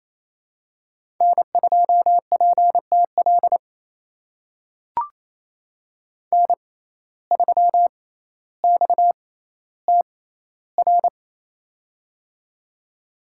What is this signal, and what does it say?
Morse code, heard as a keyed tone, about 28 wpm, spaced as for 14 wpm, 700 Hz. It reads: N2PTL N3XTR
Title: USA Call Signs - 5 Characters - 6x Character Spacing - 28wpm